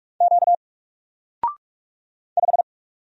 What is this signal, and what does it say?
Morse code, heard as a keyed tone, about 46 wpm, 700 Hz. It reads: Q 5
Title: Single Letter-Number 45wpm